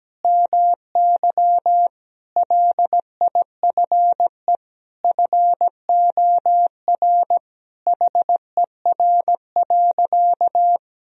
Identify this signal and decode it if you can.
Morse code, heard as a keyed tone, about 17 wpm, 700 Hz. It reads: MY LIFE FOR HER.
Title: Sentences from Top 1000 Words - Review Part 2 - 17wpm